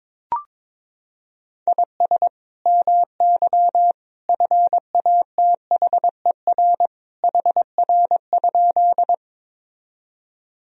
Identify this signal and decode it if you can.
Morse code, heard as a keyed tone, about 22 wpm, 700 Hz. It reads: IS MY FATHER HR?
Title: Sentences from Top 300 Words and Abbreviations 22wpm